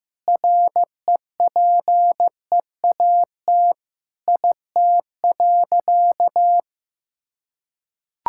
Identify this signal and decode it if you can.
Morse code, heard as a keyed tone, about 15 wpm, 700 Hz. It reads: REPEAT IT.
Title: Sentences from Top 1600 Words 15wpm